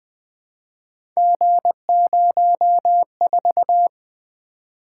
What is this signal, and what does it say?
Morse code, heard as a keyed tone, about 20 wpm, 700 Hz. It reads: G04
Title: Three Letter-Number Combinations - 20wpm